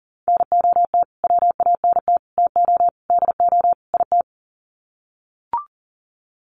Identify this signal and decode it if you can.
Morse code, heard as a keyed tone, about 40 wpm, 700 Hz. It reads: NOT WANT TO DO IT
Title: Sets of 5 Words Encore 40wpm